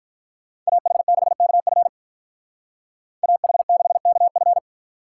Morse code, heard as a keyed tone, about 43 wpm, 700 Hz. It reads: AH6KF AH6KF
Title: Speed Racing - USA 2x2 Call Signs 35wpm